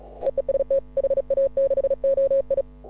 Morse code, heard as a keyed tone, about 35 wpm, 530 Hz. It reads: EE ST HA6OI
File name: AUDIO2018 / HA6OI CW